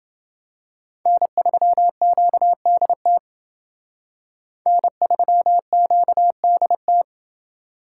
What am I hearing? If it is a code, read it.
Morse code, heard as a keyed tone, about 29 wpm, 700 Hz. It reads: N3QDT N3QDT